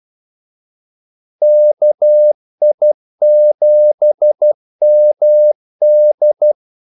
Morse code, heard as a keyed tone, about 12 wpm, 600 Hz. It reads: KI7MD
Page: 2 ekstra ordmellemrum mellem kaldesignalerne